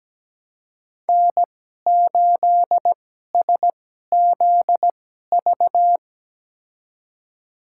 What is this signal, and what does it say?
Morse code, heard as a keyed tone, about 17 wpm, spaced as for 12 wpm, 700 Hz. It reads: N8SZV
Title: USA Call Signs - 5 Characters - 2x Character Spacing - 17wpm